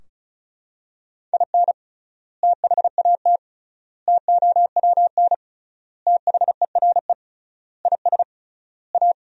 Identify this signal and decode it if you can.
Morse code, heard as a keyed tone, about 36 wpm, 700 Hz. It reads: IN THAT TOWN THERE IS A
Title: 3x Word-Spacing - Sentences from Top 500 Words - 35wpm